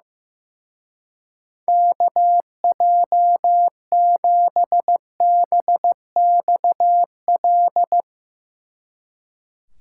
Morse code, heard as a keyed tone, about 15 wpm, 700 Hz. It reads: KJ7BXL
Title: Speed Racing - USA 2x3 Call Signs 15wpm